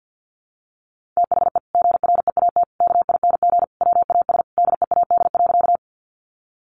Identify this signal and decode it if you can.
Morse code, heard as a keyed tone, about 50 wpm, 700 Hz. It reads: THE GREAT KING WAS DEAD.
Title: Sentences from Top 1500 Words 50wpm